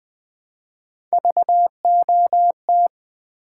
Morse code, heard as a keyed tone, about 20 wpm, 700 Hz. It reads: VOT
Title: Build A Word - 1 Letter At A Time - 3 to 5-Letter Words - 20wpm